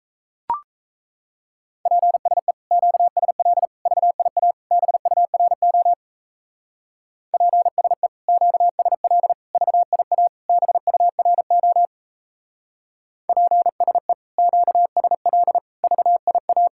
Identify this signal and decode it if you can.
Morse code, heard as a keyed tone, about 32 wpm, 700 Hz. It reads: PSE QSL VIA BURO PSE QSL VIA BURO PSE QSL VIA
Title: Speed-Racing Common QSO Phrases - 28wpm